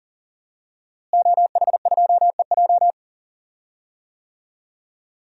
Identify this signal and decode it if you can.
Morse code, heard as a keyed tone, about 40 wpm, 700 Hz. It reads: OH2EJ
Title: DXCC-100 Call Signs - 5 Characters - Repeated 3x - 40wpm